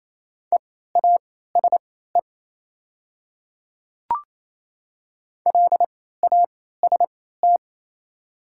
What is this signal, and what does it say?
Morse code, heard as a keyed tone, about 28 wpm, spaced as for 16 wpm, 700 Hz. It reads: EASE LAST
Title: 3x Character Spacing - Top 300 Words - 28wpm